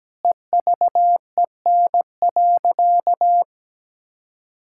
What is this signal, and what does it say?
Morse code, heard as a keyed tone, about 17 wpm, 700 Hz. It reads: EVEN.
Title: Sentences from Top 1600 Words 17wpm